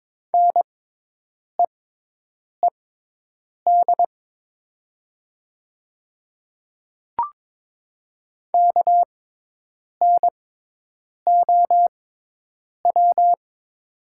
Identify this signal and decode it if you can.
Morse code, heard as a keyed tone, about 22 wpm, spaced as for 8 wpm, 700 Hz. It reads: NEED KNOW
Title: 6x Character Spacing - Top 300 Words - 22wpm